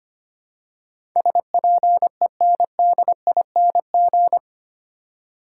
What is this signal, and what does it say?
Morse code, heard as a keyed tone, about 25 wpm, 700 Hz. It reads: SPENDING
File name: 8 Letter Words - First 3 Letters Given - 25wpm